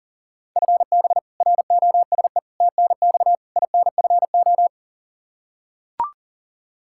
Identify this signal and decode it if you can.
Morse code, heard as a keyed tone, about 37 wpm, 700 Hz. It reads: FB ROSE TNX INFO
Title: Practice QSO - Exchange 3 - More Info - 40wpm